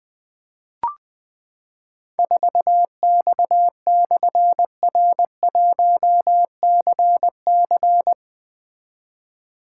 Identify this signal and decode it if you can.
Morse code, heard as a keyed tone, about 20 wpm, 700 Hz. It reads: 4X/R1CC
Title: Difficult Call Signs 20wpm